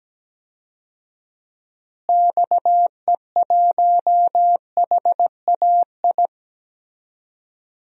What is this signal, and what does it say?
Morse code, heard as a keyed tone, about 17 wpm, 700 Hz. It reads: XE1HAI
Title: DXCC-100 Call Signs - 6 Characters - Repeated 3x - 17wpm